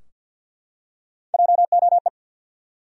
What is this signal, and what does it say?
Morse code, heard as a keyed tone, about 50 wpm, 700 Hz. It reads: JOE
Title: USA Common Names - Max 4 Letters - 50wpm